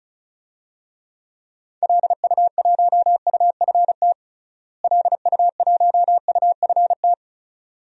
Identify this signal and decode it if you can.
Morse code, heard as a keyed tone, about 36 wpm, 700 Hz. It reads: LU1UFT LU1UFT
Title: DXCC-100 Call Signs - 6 Characters - Repeated 2x - 35wpm